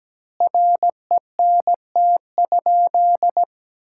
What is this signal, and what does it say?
Morse code, heard as a keyed tone, about 17 wpm, 700 Hz. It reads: RENT?
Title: Sentences from Top 1000 Words - Review Part 5 - 17wpm